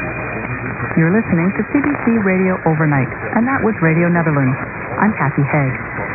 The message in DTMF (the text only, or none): none